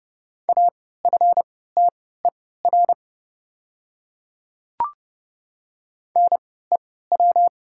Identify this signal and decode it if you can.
Morse code, heard as a keyed tone, about 30 wpm, spaced as for 17 wpm, 690 Hz. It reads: AFTER NEW
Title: 3x Character Spacing - Top 200 Words - 30wpm